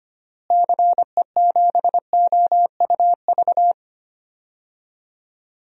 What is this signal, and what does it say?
Morse code, heard as a keyed tone, about 25 wpm, 700 Hz. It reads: CE7OUV
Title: DXCC-100 Call Signs - 6 Characters - Repeated 3x - 25wpm